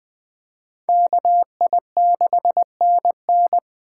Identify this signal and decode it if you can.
Morse code, heard as a keyed tone, about 20 wpm, 700 Hz. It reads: KI6NN